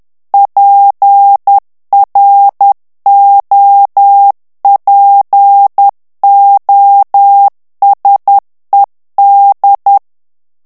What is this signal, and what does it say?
Morse code, heard as a keyed tone, about 10 wpm, 800 Hz. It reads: PROPOSED
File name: ANGLAIS texte clair 3 vitesse 10 mots minute